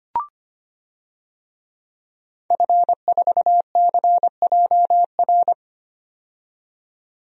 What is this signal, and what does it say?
Morse code, heard as a keyed tone, about 25 wpm, 700 Hz. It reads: F4CJR
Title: DXCC-100 Call Signs - 5 Characters - 25wpm